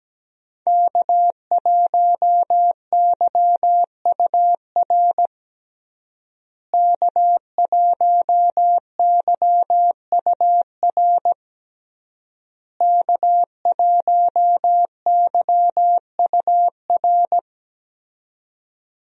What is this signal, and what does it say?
Morse code, heard as a keyed tone, about 17 wpm, 700 Hz. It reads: K1YUR K1YUR K1YUR
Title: USA Call Signs - 5 Characters - Repeated 3x - 17wpm